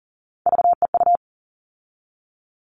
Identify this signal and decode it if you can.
Morse code, heard as a keyed tone, about 40 wpm, 700 Hz. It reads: VEU
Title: All Cut Numbers - 3-Digit 40wpm